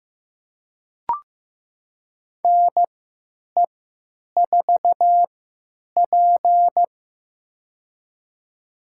Morse code, heard as a keyed tone, about 15 wpm, spaced as for 9 wpm, 700 Hz. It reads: NE4P
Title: USA Call Signs - 4 Characters - 3x Character Spacing - 15wpm